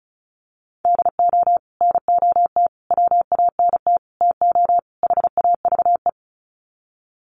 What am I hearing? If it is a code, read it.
Morse code, heard as a keyed tone, about 34 wpm, 700 Hz. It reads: DO NOT WANT TO HAVE
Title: Sets of 5 Words 35wpm